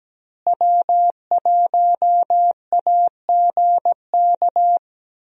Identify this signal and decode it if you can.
Morse code, heard as a keyed tone, about 17 wpm, 700 Hz. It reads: W1AGK
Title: USA Call Signs - 5 Characters - Repeated 2x - 17wpm